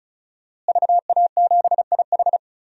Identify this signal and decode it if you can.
Morse code, heard as a keyed tone, about 36 wpm, 700 Hz. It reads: VA7IH
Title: DXCC-100 Call Signs - 5 Characters - Repeated 2x - 35wpm